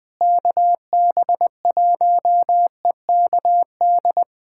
Morse code, heard as a keyed tone, about 21 wpm, 700 Hz. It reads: KB1EKD
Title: USA Callsign DE Callsign - 6 Characters - 20wpm